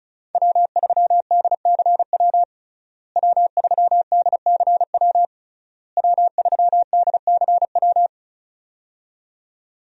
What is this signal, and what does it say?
Morse code, heard as a keyed tone, about 36 wpm, 700 Hz. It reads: W3DCW W3DCW W3DCW
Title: USA Call Signs - 5 Characters - Repeated 3x - 35wpm